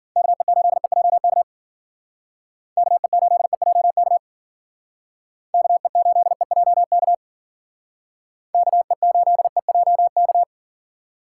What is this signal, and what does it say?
Morse code, heard as a keyed tone, about 50 wpm, 700 Hz. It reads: KE8EJK KE8EJK KE8EJK KE8EJK